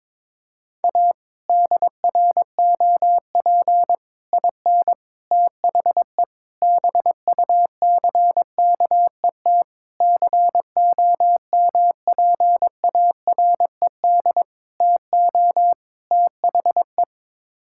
Morse code, heard as a keyed tone, about 22 wpm, 700 Hz. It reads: A DROP IN THE BUCKET COMPARED TO THE
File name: practice / 22wpm / Sets of 8 Words - 22wpm